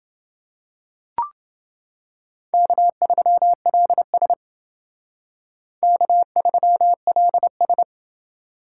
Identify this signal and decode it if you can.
Morse code, heard as a keyed tone, about 28 wpm, 700 Hz. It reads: K3LS K3LS